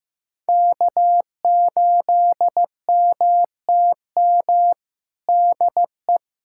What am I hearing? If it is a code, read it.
Morse code, heard as a keyed tone, about 15 wpm, 700 Hz. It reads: K8MTM DE
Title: USA Callsign DE Callsign - 5 Characters - 15wpm